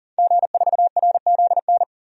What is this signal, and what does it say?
Morse code, heard as a keyed tone, about 40 wpm, 700 Hz. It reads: G4RZN
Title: DXCC-100 Call Signs - 5 Characters - 40wpm